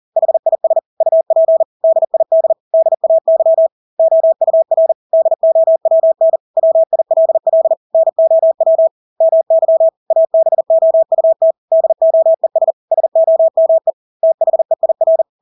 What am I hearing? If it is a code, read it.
Morse code, heard as a keyed tone, about 40 wpm, 650 Hz. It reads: HIS UP DID DAY OUR DOWN WILL NOW MY ABOUT DOES SOME THEIR
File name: Engelske ord 40-40 wpm